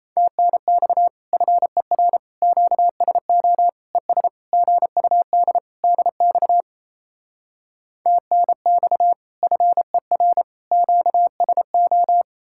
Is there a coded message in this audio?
Yes: Morse code from 0.2 to 12.2 s, heard as a keyed tone, about 38 wpm, 700 Hz. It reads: TNX FER QSO ES GUD DX TNX FER QSO